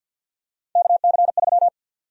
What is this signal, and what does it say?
Morse code, heard as a keyed tone, about 52 wpm, 700 Hz. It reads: KC3
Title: USA Call Sign Prefixes - 50wpm